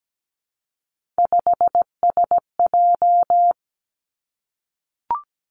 Morse code, heard as a keyed tone, about 17 wpm, 700 Hz. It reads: 5SJ
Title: Three Challenging Letter-Number Combinations 17wpm